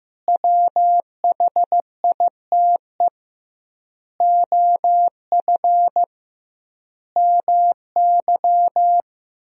Morse code, heard as a keyed tone, about 15 wpm, 700 Hz. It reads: WHITE OF MY
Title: 2x Word-Spacing - Sentences from Top 500 Words - 15wpm